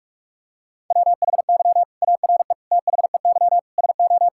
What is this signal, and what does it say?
Morse code, heard as a keyed tone, about 45 wpm, 700 Hz. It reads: WHY ARE THEY SO
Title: Sentences from Top 1000 Words - Review Part 5 - 45wpm